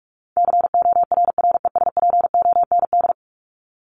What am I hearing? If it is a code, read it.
Morse code, heard as a keyed tone, about 44 wpm, 700 Hz. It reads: CORRESPOND
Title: ING Suffix - Top 100 Words with Base Words 45wpm